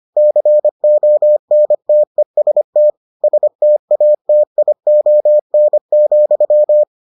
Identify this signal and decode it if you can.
Morse code, heard as a keyed tone, about 25 wpm, 600 Hz. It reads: CONTEST STATION,